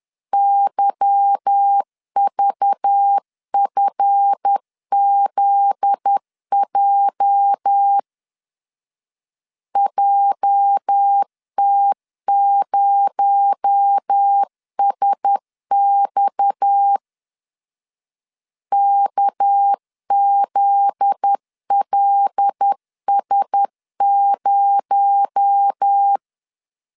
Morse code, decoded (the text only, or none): YVFZJ JT0SX KZLS0